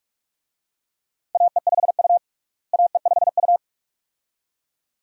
Morse code, heard as a keyed tone, about 46 wpm, 700 Hz. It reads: AE5U AE5U
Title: USA Call Signs - 4 Characters - Repeated 2x - 45wpm